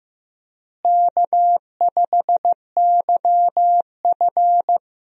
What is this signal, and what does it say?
Morse code, heard as a keyed tone, about 16 wpm, 700 Hz. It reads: K5YF
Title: USA Call Signs - 4 Characters - Repeated 2x - 15wpm